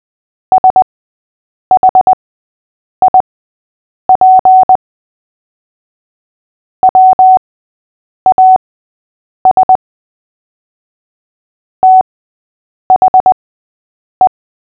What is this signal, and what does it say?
Morse code, heard as a keyed tone, about 20 wpm, spaced as for 8 wpm, 750 Hz. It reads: SHIP WAS THE
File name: cw / fists / Short-Story-3-8wpm